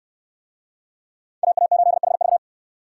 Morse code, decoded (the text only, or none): AA8HV